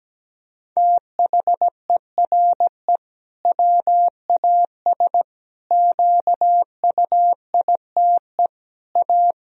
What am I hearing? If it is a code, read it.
Morse code, heard as a keyed tone, about 17 wpm, 700 Hz. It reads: THERE WAS QUITE A
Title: Sentences from Top 1000 Words - Review Part 1 - 17wpm